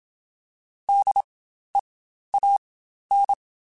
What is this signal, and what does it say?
Morse code, heard as a keyed tone, about 27 wpm, spaced as for 12 wpm, 800 Hz. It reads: DEAN
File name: Spaced Alphas / Spaced Alphas mp3s / Ds sp @25